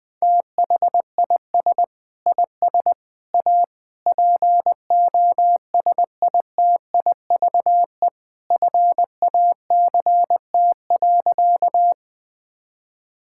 Morse code, heard as a keyed tone, about 20 wpm, 700 Hz. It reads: THIS IS A POSITIVE FACT.
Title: Sentences from Top 1000 Words - Review Part 5 - 20wpm